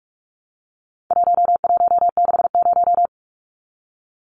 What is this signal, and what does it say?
Morse code, heard as a keyed tone, about 44 wpm, 700 Hz. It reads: 1160
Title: Numbers - 4-Digit 45wpm